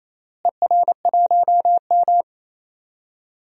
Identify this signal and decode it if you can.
Morse code, heard as a keyed tone, about 28 wpm, 700 Hz. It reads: ER1M
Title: DXCC-100 Call Signs - 4 Characters - Repeated 3x - 28wpm